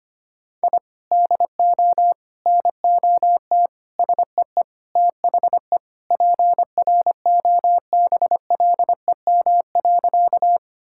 Morse code, heard as a keyed tone, about 25 wpm, 700 Hz. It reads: I DO NOT SEE THE PROBLEM.
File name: practice / 25wpm / Sentences with 4-Word Phrases - 25wpm